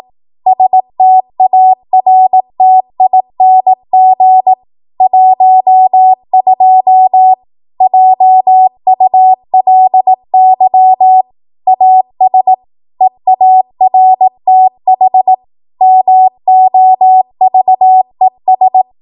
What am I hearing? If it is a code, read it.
Morse code, heard as a keyed tone, about 18 wpm, 750 Hz. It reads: STARTING 12 JULY AS EARTH MOVES
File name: Morse / 18WPM